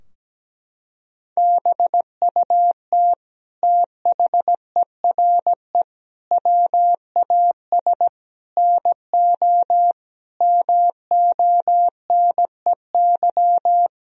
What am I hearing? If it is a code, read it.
Morse code, heard as a keyed tone, about 17 wpm, 700 Hz. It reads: BUT THERE WAS NO MONEY
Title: Sentences from Top 1000 Words - Review Part 2 - 17wpm